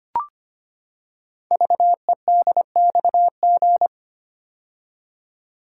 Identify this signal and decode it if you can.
Morse code, heard as a keyed tone, about 25 wpm, 700 Hz. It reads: VEDXG
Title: Code Groups - 5 Letters - 25wpm